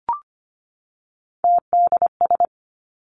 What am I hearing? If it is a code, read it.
Morse code, heard as a keyed tone, about 25 wpm, 700 Hz. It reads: TDS
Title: Three Challenging Letter-Number Combinations 25wpm